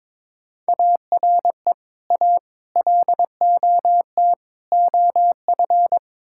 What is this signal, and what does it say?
Morse code, heard as a keyed tone, about 23 wpm, 700 Hz. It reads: ARE A LOT OF
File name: practice / 22wpm / Build A Phrase - 5-Word Phrases - 22wpm